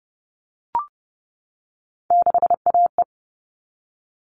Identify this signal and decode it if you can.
Morse code, heard as a keyed tone, about 30 wpm, 700 Hz. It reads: 6AE
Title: All Cut Numbers - 3-Digit 30wpm